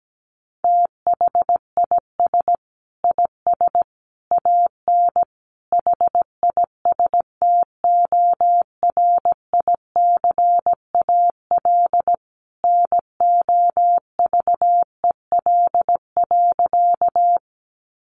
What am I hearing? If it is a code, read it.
Morse code, heard as a keyed tone, about 17 wpm, 700 Hz. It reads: THIS IS AN HISTORICAL NOVEL.
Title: Sentences from Top 1300 Words 17wpm